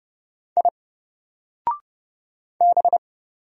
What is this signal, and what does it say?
Morse code, heard as a keyed tone, about 30 wpm, 700 Hz. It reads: I B
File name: Single-Letter-Number-30wpm